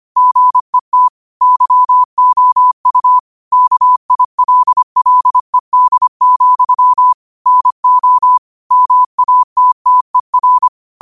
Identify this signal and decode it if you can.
Morse code, heard as a keyed tone, about 25 wpm, 1000 Hz. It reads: GET YOU KILLED, NO MATTER